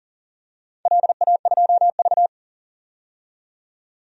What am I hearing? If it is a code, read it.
Morse code, heard as a keyed tone, about 37 wpm, 700 Hz. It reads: LA2V
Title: DXCC-100 Call Signs - 4 Characters - 40wpm